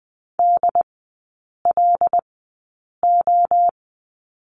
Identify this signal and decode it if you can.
Morse code, heard as a keyed tone, about 20 wpm, spaced as for 8 wpm, 700 Hz. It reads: DLO
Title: 205 Lesson 33e - Letter j - Three Characters 20wpm